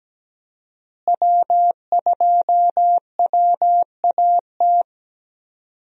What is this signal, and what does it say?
Morse code, heard as a keyed tone, about 17 wpm, 700 Hz. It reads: W2WAT
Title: USA Call Signs - 5 Characters - Repeated 3x - 17wpm